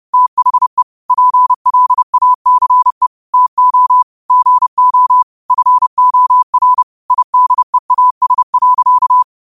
Morse code, heard as a keyed tone, about 30 wpm, 1000 Hz. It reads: THE PLACE TO GO FOR IDEAS.
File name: qotdfiles / 30wpm